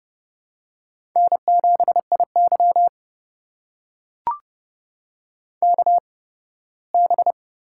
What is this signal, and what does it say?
Morse code, heard as a keyed tone, about 31 wpm, spaced as for 18 wpm, 700 Hz. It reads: N7IY KB
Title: USA Call Signs - 4 Characters - 8x Character Spacing - 30wpm